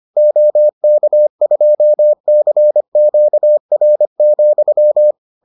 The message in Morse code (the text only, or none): OK2CQR,